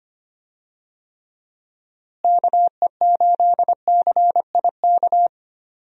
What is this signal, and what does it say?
Morse code, heard as a keyed tone, about 25 wpm, 700 Hz. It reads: KE8CIK
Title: USA Call Signs - 6 Characters - Repeated 3x - 25wpm